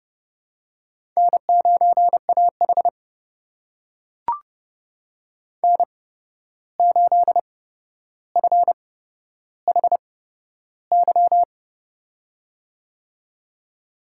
Morse code, heard as a keyed tone, about 30 wpm, spaced as for 12 wpm, 700 Hz. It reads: N9AH N8FHY
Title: USA Call Signs - 8x Character Spacing - 30wpm